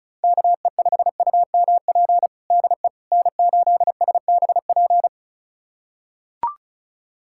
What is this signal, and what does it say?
Morse code, heard as a keyed tone, about 36 wpm, 700 Hz. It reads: KE5UMP DE N8SBP
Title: USA Callsign DE Callsign 35wpm